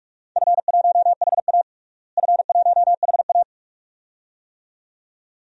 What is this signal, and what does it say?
Morse code, heard as a keyed tone, about 46 wpm, 700 Hz. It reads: F1HA F1HA
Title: DXCC-100 Call Signs - 4 Characters - Repeated 2x - 45wpm